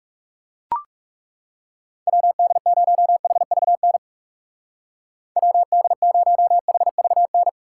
Morse code, heard as a keyed tone, about 36 wpm, 700 Hz. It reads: WD0HVN WD0HVN
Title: Speed Racing - USA Call Signs 30wpm